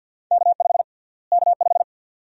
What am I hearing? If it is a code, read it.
Morse code, heard as a keyed tone, about 51 wpm, 700 Hz. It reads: K5 K5
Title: USA Call Sign Prefixes - Repeated 2x - 50wpm